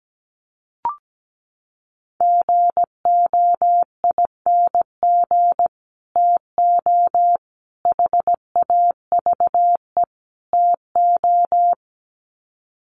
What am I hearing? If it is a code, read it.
Morse code, heard as a keyed tone, about 17 wpm, 700 Hz. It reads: GOING TO HAVE TO
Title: Sets of 4 Words 17wpm